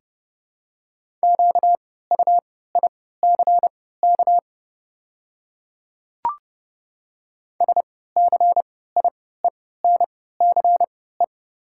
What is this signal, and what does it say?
Morse code, heard as a keyed tone, about 30 wpm, spaced as for 17 wpm, 690 Hz. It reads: QUICK SCIENCE
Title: 3x Character Spacing - Top 500 Words - 30wpm